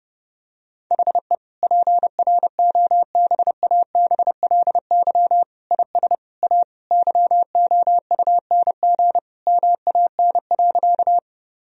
Morse code, heard as a keyed tone, about 30 wpm, 700 Hz. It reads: HE PROBABLY IS A YOUNG MAN.
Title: Sentences from Top 1000 Words - Review Part 4 - 30wpm